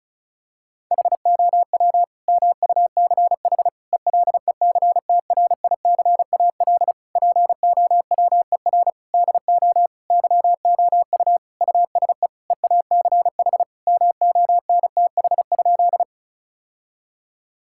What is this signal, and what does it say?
Morse code, heard as a keyed tone, about 34 wpm, 700 Hz. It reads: HOW MUCH ELECTRICAL POWER DO YOU USE EACH MONTH?
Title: Sentences from Top 1000 Words - Review Part 6 - 35wpm